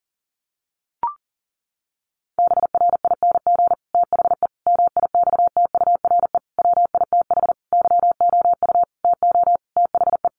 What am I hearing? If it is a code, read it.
Morse code, heard as a keyed tone, about 40 wpm, 700 Hz. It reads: BRING THE MIXTURE WITH YOU TO THE
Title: Sentences from Top 1500 Words 40wpm